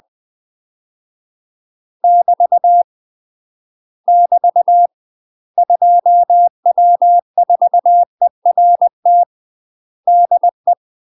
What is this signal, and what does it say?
Morse code, heard as a keyed tone, about 20 wpm, 700 Hz. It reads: <BT> <BT> 2W4ERT DE
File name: UK MASTER QSOs TRAINING / 20WPM / UK MASTER QSOs 0040